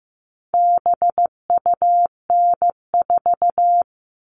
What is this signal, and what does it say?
Morse code, heard as a keyed tone, about 15 wpm, 700 Hz. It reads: BUN4
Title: All Cut Numbers - 4-Digit 15wpm